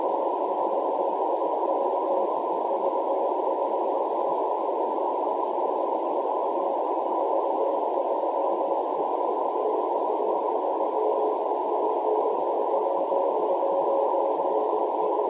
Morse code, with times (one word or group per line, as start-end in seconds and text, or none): none